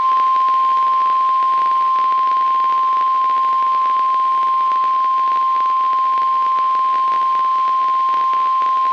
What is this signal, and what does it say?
Emergency Alert System (SAME) tones (the 1050 Hz attention tone). It is heard from 0.4 to 8.2 s.
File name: eas messages / EAS WXM95 Towanda